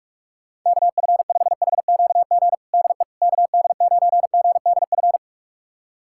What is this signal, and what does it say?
Morse code, heard as a keyed tone, about 48 wpm, 700 Hz. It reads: KF5HQG DE KD9GDF
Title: USA Callsign DE Callsign 45wpm